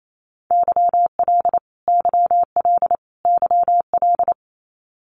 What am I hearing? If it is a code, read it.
Morse code, heard as a keyed tone, about 28 wpm, 700 Hz. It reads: YL YL YL
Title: Abbreviations Repeated 3x - 28wpm